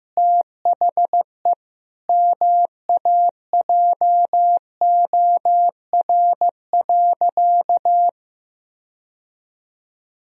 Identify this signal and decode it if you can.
Morse code, heard as a keyed tone, about 15 wpm, 700 Hz. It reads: THE MAJOR.
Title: Sentences from Top 1000 Words - Review Part 4 - 15wpm